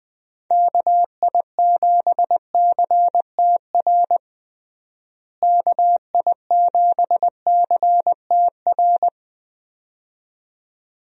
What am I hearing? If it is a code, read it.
Morse code, heard as a keyed tone, about 20 wpm, 700 Hz. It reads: KI7CTR KI7CTR
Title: USA Call Signs - 6 Characters - Repeated 2x - 20wpm